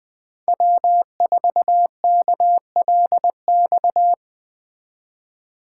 Morse code, heard as a keyed tone, about 20 wpm, 700 Hz. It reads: W4KLX
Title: Speed Racing - USA 1x3 Call Signs 17wpm